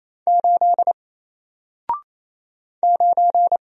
Morse code, heard as a keyed tone, about 27 wpm, 700 Hz. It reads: 8 9
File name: Single-Letter-Number-28wpm